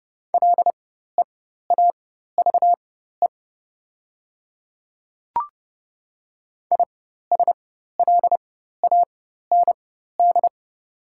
Morse code, heard as a keyed tone, about 30 wpm, spaced as for 14 wpm, 690 Hz. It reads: LEAVE ISLAND
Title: 4x Character Spacing - Top 500 Words - 30wpm